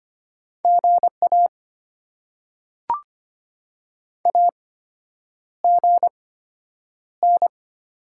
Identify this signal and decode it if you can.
Morse code, heard as a keyed tone, about 25 wpm, spaced as for 9 wpm, 700 Hz. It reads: GA AGN
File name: Top 90 QSO Elements - 8x Character Spacing - 25wpm